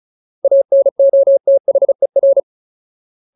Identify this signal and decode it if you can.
Morse code, heard as a keyed tone, about 34 wpm, 550 Hz. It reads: ANOTHER
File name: Ultimate Word List / 35WPM Ultimate